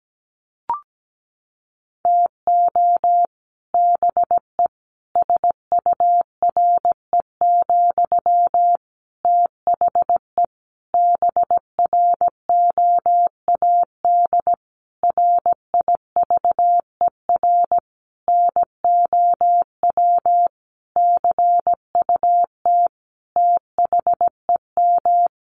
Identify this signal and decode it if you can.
Morse code, heard as a keyed tone, about 17 wpm, 700 Hz. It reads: TO BE SURE, THE BROAD RIVER NOW CUT THEM
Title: Sentences from Top 1000 Words 17wpm